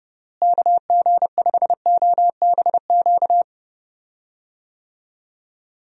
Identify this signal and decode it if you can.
Morse code, heard as a keyed tone, about 30 wpm, 700 Hz. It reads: KG5OBQ
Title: USA Call Signs - 6 Characters - Repeated 2x - 30wpm